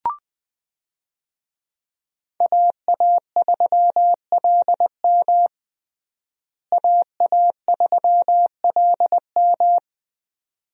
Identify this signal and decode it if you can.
Morse code, heard as a keyed tone, about 20 wpm, 700 Hz. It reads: AA3LM AA3LM
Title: USA Call Signs - 5 Characters - Repeated 2x - 20wpm